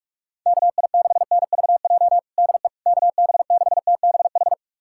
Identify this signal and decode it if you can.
Morse code, heard as a keyed tone, about 46 wpm, 690 Hz. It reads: KI6NVJ DE KB6TBH
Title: USA Callsign DE Callsign - 6 Characters - 45wpm